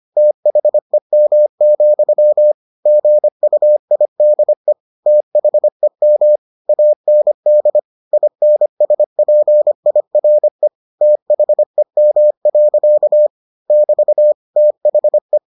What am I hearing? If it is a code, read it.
Morse code, heard as a keyed tone, about 25 wpm, 600 Hz. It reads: THEM, GUIDE THEM AND INSPIRE THEM. <BT> THE